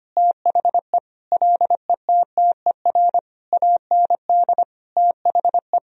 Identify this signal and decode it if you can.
Morse code, heard as a keyed tone, about 25 wpm, 700 Hz. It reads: THE LETTER AND THE
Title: Sentences from Top 1000 Words - Review Part 4 - 25wpm